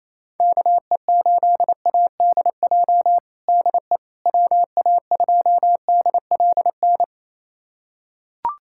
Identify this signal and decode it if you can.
Morse code, heard as a keyed tone, about 28 wpm, 700 Hz. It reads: KE8ADJ DE WA2DLN
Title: USA Callsign DE Callsign - 6 Characters - 28wpm